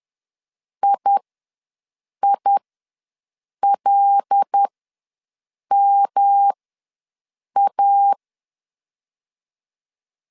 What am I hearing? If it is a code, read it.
Morse code, heard as a keyed tone, about 10 wpm, spaced as for 6 wpm, 800 Hz. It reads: IILMA
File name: LEZIONI / 5 LBVC / 5 L ws 10wpm 3636